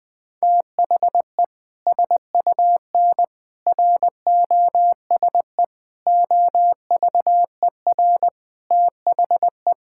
Morse code, heard as a keyed tone, about 20 wpm, 700 Hz. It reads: THE SUN ROSE OVER THE
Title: Sentences with 2-Word Phrases - 20wpm